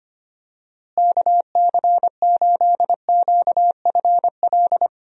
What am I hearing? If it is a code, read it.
Morse code, heard as a keyed tone, about 25 wpm, 700 Hz. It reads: KC8QFL
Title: USA Call Signs - 6 Characters - Repeated 3x - 25wpm